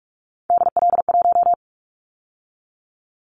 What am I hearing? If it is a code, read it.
Morse code, heard as a keyed tone, about 46 wpm, 700 Hz. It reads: DL1
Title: Callsign Prefixes for Top 50 DXCC - 45wpm